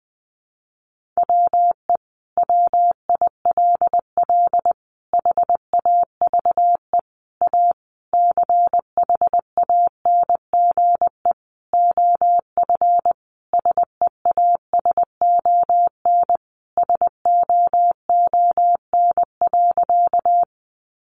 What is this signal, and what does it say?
Morse code, heard as a keyed tone, about 20 wpm, 700 Hz. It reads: WE WILL HAVE A CHANGE OF SEASON SOON.
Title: Sentences from Top 1400 Words 20wpm